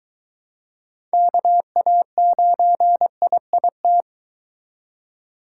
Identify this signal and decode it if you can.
Morse code, heard as a keyed tone, about 23 wpm, 700 Hz. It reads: KA9IIT